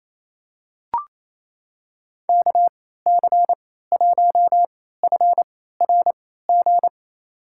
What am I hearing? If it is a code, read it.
Morse code, heard as a keyed tone, about 28 wpm, spaced as for 16 wpm, 700 Hz. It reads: KC1FRG
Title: USA Call Signs - 6 Characters - 3x Character Spacing - 28wpm